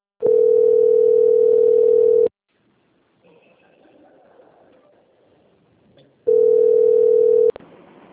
Ringback tone (North America), 440 + 480 Hz.